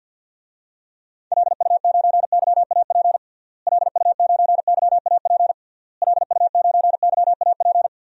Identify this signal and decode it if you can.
Morse code, heard as a keyed tone, about 49 wpm, 700 Hz. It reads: LU9YAP LU9YAP LU9YAP
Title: DXCC-100 Call Signs - 6 Characters - Repeated 3x - 50wpm